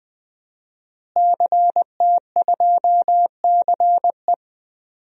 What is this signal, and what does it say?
Morse code, heard as a keyed tone, about 21 wpm, 700 Hz. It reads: CT2CE
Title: DXCC-100 Call Signs - 5 Characters - Repeated 3x - 20wpm